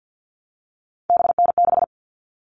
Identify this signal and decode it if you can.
Morse code, heard as a keyed tone, about 50 wpm, 700 Hz. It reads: BN6
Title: All Cut Numbers - 3-Digit 50wpm